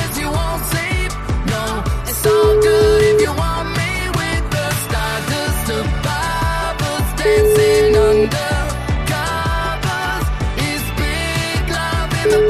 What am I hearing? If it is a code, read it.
Ringback tone (Europe, 425 Hz), with noise.